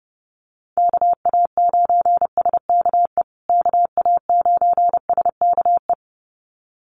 Morse code, heard as a keyed tone, about 30 wpm, 700 Hz. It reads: KA9SKE KA9SKE
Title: USA Call Signs Repeated 2x 30wpm